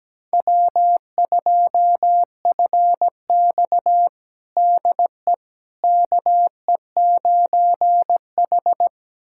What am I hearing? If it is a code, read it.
Morse code, heard as a keyed tone, about 17 wpm, 700 Hz. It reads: W2FX DE KE9H
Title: USA Callsign DE Callsign - 4 Characters - 17wpm